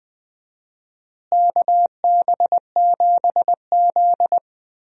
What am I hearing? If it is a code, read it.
Morse code, heard as a keyed tone, about 20 wpm, 700 Hz. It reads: KB7Z
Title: USA Call Signs - 4 Characters - Repeated 2x - 20wpm